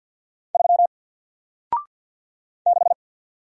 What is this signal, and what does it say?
Morse code, heard as a keyed tone, about 48 wpm, 700 Hz. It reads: 3 6
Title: Single Letter-Number Alt 50wpm